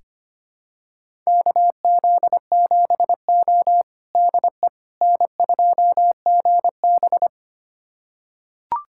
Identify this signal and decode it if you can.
Morse code, heard as a keyed tone, about 25 wpm, 700 Hz. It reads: KZ7O DE N2GB
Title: USA Callsign DE Callsign - 4 Characters - 25wpm